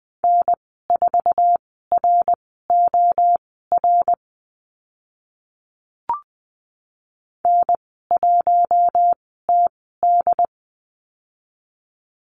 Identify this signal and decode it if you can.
Morse code, heard as a keyed tone, about 20 wpm, spaced as for 14 wpm, 700 Hz. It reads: N4ROR N1TD
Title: 082 Lesson 14g - Number 2 - Call Signs 20wpm